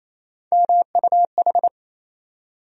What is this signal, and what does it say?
Morse code, heard as a keyed tone, about 28 wpm, 700 Hz. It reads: MUH
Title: USA Call Sign Postfixes - 3 Letters 28wpm